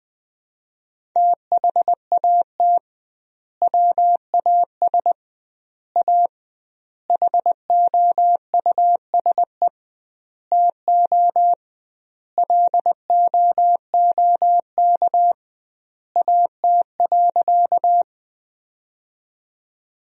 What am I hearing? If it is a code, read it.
Morse code, heard as a keyed tone, about 20 wpm, 700 Hz. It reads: THAT WAS A HOUSE TO LOOK AT.
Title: 2x Word-Spacing - Sentences from Top 200 Words - 20wpm